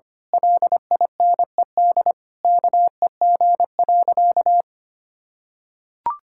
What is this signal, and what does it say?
Morse code, heard as a keyed tone, about 25 wpm, 700 Hz. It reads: LINED KEG.
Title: Pangram Challenge v2 - 25wpm